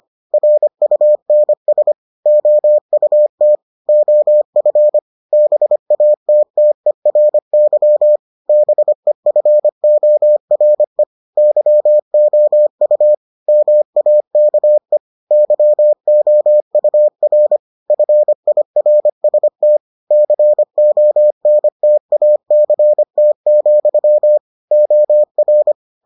Morse code, heard as a keyed tone, about 25 wpm, 600 Hz. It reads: RUNS OUT OF BATTERY BEFORE YOU MAKE YOUR FIRST CONTACT, OR